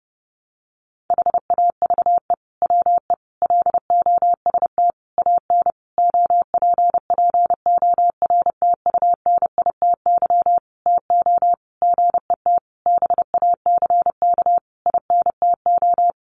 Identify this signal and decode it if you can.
Morse code, heard as a keyed tone, about 28 wpm, 700 Hz. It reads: HAVE WE LOST AN OPPORTUNITY TO GET BACK INTO
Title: Sentences from Top 1400 Words 30wpm